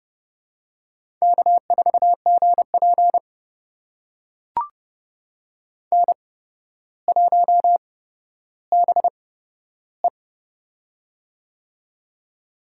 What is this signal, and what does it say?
Morse code, heard as a keyed tone, about 30 wpm, spaced as for 13 wpm, 700 Hz. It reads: K4GP N1BE
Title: USA Call Signs - 4 Characters - 8x Character Spacing - 30wpm